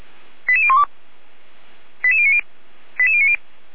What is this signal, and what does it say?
Five-tone selective call (ZVEI), heard as quick runs of five tones, tones of about 70 ms. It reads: 88012 88090 88090
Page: /455/SDR/ZVEI/